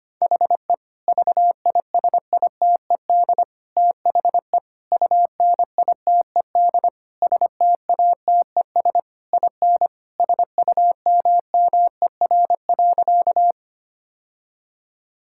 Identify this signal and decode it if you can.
Morse code, heard as a keyed tone, about 25 wpm, 700 Hz. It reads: HE VISITED THE UNITED STATES IN SUMMER.
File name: practice / 25wpm / Sentences with 4-Word Phrases - 25wpm